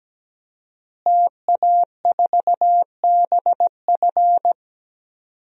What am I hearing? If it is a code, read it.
Morse code, heard as a keyed tone, about 17 wpm, 700 Hz. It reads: TA4BF
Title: DXCC-100 Call Signs - 5 Characters - Repeated 2x - 17wpm